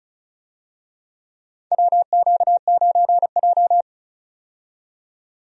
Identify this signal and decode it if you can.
Morse code, heard as a keyed tone, about 36 wpm, 700 Hz. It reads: WQ9J
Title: USA Call Signs - 4 Characters - 35wpm